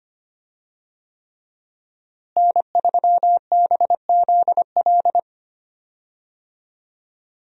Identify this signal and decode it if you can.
Morse code, heard as a keyed tone, about 25 wpm, 700 Hz. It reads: N3BZL
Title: USA Call Signs - 5 Characters - Repeated 3x - 25wpm